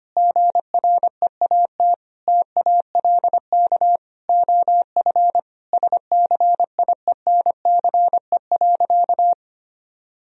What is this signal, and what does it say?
Morse code, heard as a keyed tone, about 25 wpm, 700 Hz. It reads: GREAT TALK OF SCIENCE.
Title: Sentences from Top 1000 Words - Review Part 2 - 25wpm